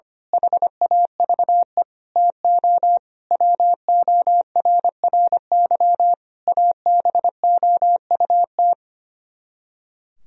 Morse code, heard as a keyed tone, about 24 wpm, 700 Hz. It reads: HAVE TO WORRY ABOUT
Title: Speed-Racing - Sets of 5 Words 25wpm